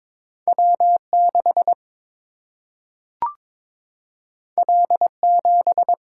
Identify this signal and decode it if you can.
Morse code, heard as a keyed tone, about 22 wpm, 700 Hz. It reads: W 6 L 7
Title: Single-Letter Single-Number 22wpm